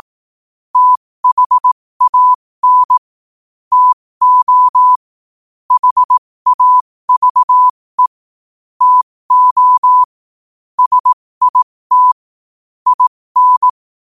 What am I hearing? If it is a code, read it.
Morse code, heard as a keyed tone, about 18 wpm, spaced as for 15 wpm, 1000 Hz. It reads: THAN TO HAVE TO SIT IN
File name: qotdfiles / 15wpm